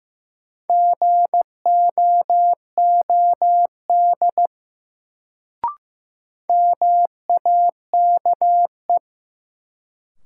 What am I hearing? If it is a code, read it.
Morse code, heard as a keyed tone, about 15 wpm, 700 Hz. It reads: GOOD MAKE
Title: Sets of 2 Words - New Word Drill - 15wpm